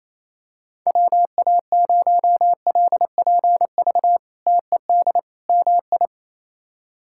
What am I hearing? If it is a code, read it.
Morse code, heard as a keyed tone, about 27 wpm, 700 Hz. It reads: WA0LPV TED MI
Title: K1USN SST QSO Practice - 28wpm